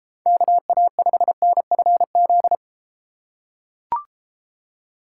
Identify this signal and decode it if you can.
Morse code, heard as a keyed tone, about 31 wpm, 700 Hz. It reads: KA5NFZ